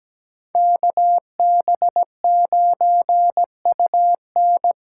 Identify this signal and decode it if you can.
Morse code, heard as a keyed tone, about 17 wpm, 700 Hz. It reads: KB9UN
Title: USA Call Signs - 6 Characters - 8x Character Spacing - 17wpm